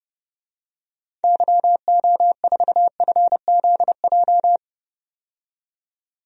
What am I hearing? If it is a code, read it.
Morse code, heard as a keyed tone, about 30 wpm, 700 Hz. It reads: YO4FZJ
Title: DXCC-100 Call Signs - 6 Characters - 30wpm